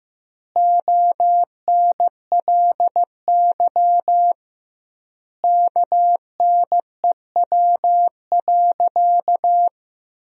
Morse code, heard as a keyed tone, about 15 wpm, 700 Hz. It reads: ONLY KNEW.
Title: 2x Word-Spacing - Sentences from Top 500 Words - 15wpm